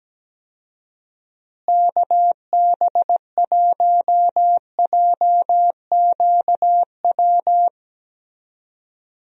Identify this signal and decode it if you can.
Morse code, heard as a keyed tone, about 17 wpm, 700 Hz. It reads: KB1JQW
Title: USA Call Signs - 6 Characters - Repeated 3x - 17wpm